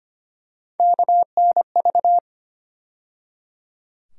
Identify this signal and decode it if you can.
Morse code, heard as a keyed tone, about 25 wpm, 700 Hz. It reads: KNV
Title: USA Call Sign Postfixes - 3 Letters 25wpm